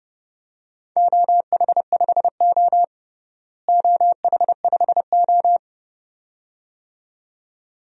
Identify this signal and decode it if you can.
Morse code, heard as a keyed tone, about 30 wpm, 700 Hz. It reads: OH5O OH5O
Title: DXCC-100 Call Signs - 4 Characters - Repeated 2x - 30wpm